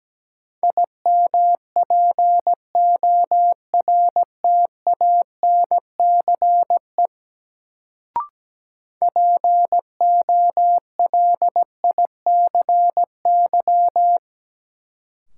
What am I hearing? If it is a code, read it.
Morse code, heard as a keyed tone, about 17 wpm, 700 Hz. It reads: IMPORTANCE POLICY
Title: Sentences from Top 1200 Words - New Word Drill Alt - 17wpm